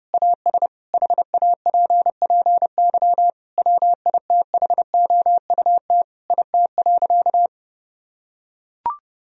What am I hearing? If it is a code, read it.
Morse code, heard as a keyed tone, about 30 wpm, 700 Hz. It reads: AS HAPPY WITHOUT IT.
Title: Sentences from Top 1000 Words - Review Part 3 - 30wpm